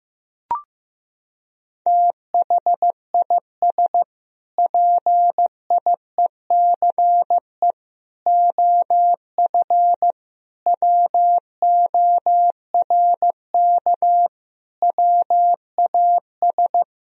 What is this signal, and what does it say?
Morse code, heard as a keyed tone, about 16 wpm, 700 Hz. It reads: THIS PIECE OF WORK WAS
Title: Sentences from Top 1000 Words - Review Part 2 - 15wpm